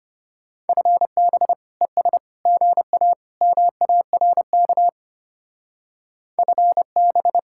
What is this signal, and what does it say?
Morse code, heard as a keyed tone, about 31 wpm, 700 Hz. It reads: FB ES GA MARK FB
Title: Speed-Racing Common QSO Phrases - 25wpm